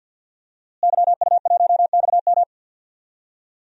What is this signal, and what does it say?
Morse code, heard as a keyed tone, about 45 wpm, 700 Hz. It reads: YU1XM
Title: DXCC-100 Call Signs - 5 Characters - Repeated 3x - 50wpm